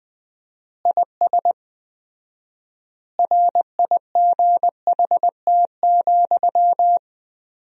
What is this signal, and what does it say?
Morse code, heard as a keyed tone, about 20 wpm, 700 Hz. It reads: IS RIGHT,
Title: 4x Word-Spacing - Sentences from Top 200 Words - 20wpm